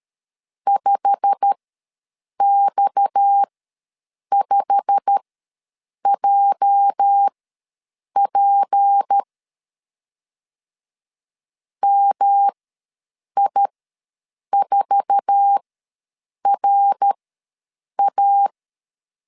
Morse code, heard as a keyed tone, about 13 wpm, spaced as for 7 wpm, 800 Hz. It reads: 5X5JP MI4RA
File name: LEZIONI / 8 YJ45 / 8 YJ45 ws 12wpm 3146